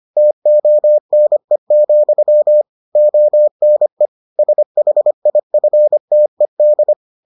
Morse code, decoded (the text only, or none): TONE, ONE SHIFTED